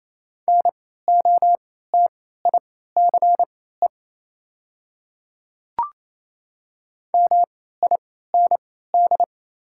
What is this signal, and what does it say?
Morse code, heard as a keyed tone, about 28 wpm, spaced as for 16 wpm, 700 Hz. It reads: NOTICE MIND
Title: 3x Character Spacing - Top 500 Words - 28wpm